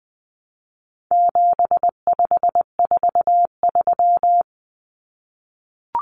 Morse code, heard as a keyed tone, about 20 wpm, 700 Hz. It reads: 7543
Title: Common Cut Numbers - 4-Digit 20wpm